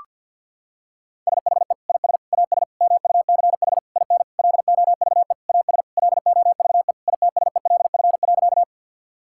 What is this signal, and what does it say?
Morse code, heard as a keyed tone, about 50 wpm, 700 Hz. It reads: SHE IS AS MUCH IN LOVE AS LOVE ITSELF.
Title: Sentences from Top 1000 Words - Review Part 5 - 50wpm